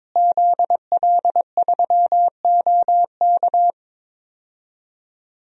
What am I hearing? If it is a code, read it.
Morse code, heard as a keyed tone, about 22 wpm, 700 Hz. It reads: ZL3OK
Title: DXCC-100 Call Signs - 5 Characters - Repeated 2x - 22wpm